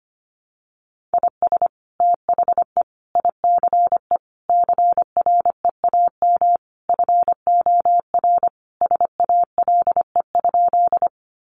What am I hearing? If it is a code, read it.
Morse code, heard as a keyed tone, about 25 wpm, 700 Hz. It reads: IS THE ICE CREAM FOR SALE?
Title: Sentences from Top 1500 Words 25wpm